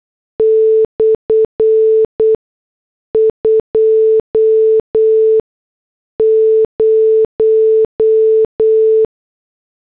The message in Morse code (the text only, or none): /20